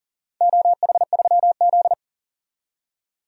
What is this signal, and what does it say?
Morse code, heard as a keyed tone, about 40 wpm, 700 Hz. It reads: OH3Z
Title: DXCC-100 Call Signs - 4 Characters - 40wpm